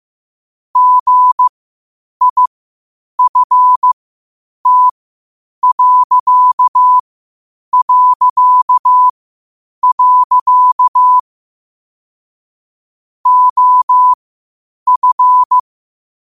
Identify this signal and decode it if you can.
Morse code, heard as a keyed tone, about 15 wpm, spaced as for 8 wpm, 1000 Hz. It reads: GIFT... OF
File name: qotdfiles / 8wpm